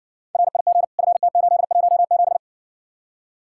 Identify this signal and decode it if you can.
Morse code, heard as a keyed tone, about 60 wpm, 700 Hz. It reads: RIG FT817